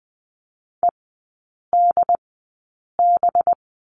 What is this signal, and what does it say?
Morse code, heard as a keyed tone, about 20 wpm, spaced as for 8 wpm, 700 Hz. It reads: EDB
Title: 184 Lesson 30e - Letter b - Three Characters 20wpm